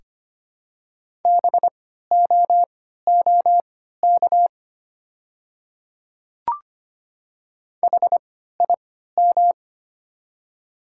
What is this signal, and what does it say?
Morse code, heard as a keyed tone, about 25 wpm, spaced as for 14 wpm, 700 Hz. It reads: BOOK HIM
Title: 3x Character Spacing - Top 300 Words - 25wpm